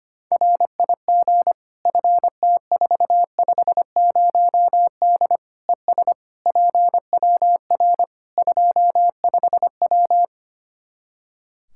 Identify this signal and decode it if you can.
Morse code, heard as a keyed tone, about 25 wpm, 700 Hz. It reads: RIG FT450D ES PWR 25W
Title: Practice QSO - Exchange 2 - Info - 25wpm